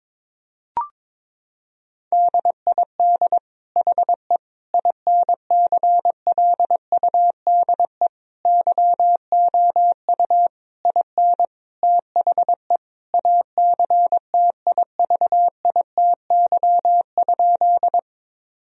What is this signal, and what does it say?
Morse code, heard as a keyed tone, about 22 wpm, 700 Hz. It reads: DID HE INCLUDE YOU IN THE ACTIVITY?
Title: Sentences from Top 1000 Words - Review Part 4 - 22wpm